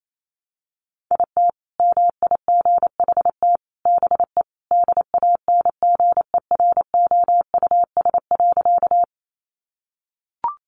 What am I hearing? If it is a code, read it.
Morse code, heard as a keyed tone, about 28 wpm, 700 Hz. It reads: IT MIGHT BE DANGEROUS.
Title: Sentences from Top 1200 Words 28wpm